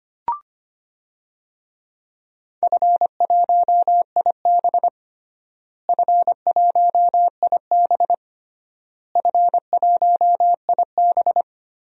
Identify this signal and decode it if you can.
Morse code, heard as a keyed tone, about 25 wpm, 700 Hz. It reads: F1IB F1IB F1IB
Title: DXCC-100 Call Signs - 4 Characters - Repeated 3x - 25wpm